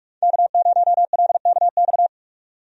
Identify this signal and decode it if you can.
Morse code, heard as a keyed tone, about 41 wpm, 700 Hz. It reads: K0LKX
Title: USA Call Signs - 5 Characters - 45wpm